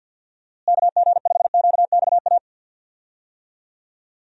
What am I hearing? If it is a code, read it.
Morse code, heard as a keyed tone, about 51 wpm, 700 Hz. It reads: KG5QXA
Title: USA Call Signs - 6 Characters - Repeated 3x - 50wpm